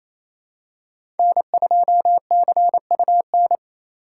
Morse code, heard as a keyed tone, about 28 wpm, 700 Hz. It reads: N2CUN
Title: USA Call Signs - 5 Characters - 6x Character Spacing - 28wpm